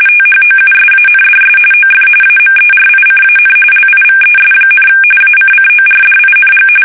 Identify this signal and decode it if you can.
RTTY, about 45.45 baud. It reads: THOUSANDS OF AMERICAN LIVES, ACCORDING
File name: fsk850 / battleshipnj